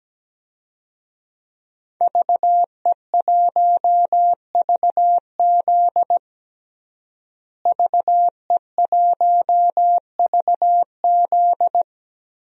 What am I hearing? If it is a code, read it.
Morse code, heard as a keyed tone, about 17 wpm, 700 Hz. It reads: VE1VZ VE1VZ
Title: DXCC-100 Call Signs - 5 Characters - Repeated 3x - 17wpm